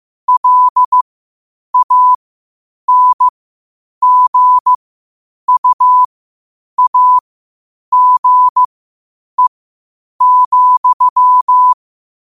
Morse code, heard as a keyed tone, about 15 wpm, spaced as for 8 wpm, 1000 Hz. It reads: LANGUAGE,
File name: qotdfiles / 8wpm